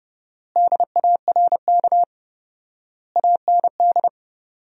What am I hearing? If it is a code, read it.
Morse code, heard as a keyed tone, about 30 wpm, 700 Hz. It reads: DARK AND
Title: 4x Word-Spacing - Sentences from Top 500 Words - 30wpm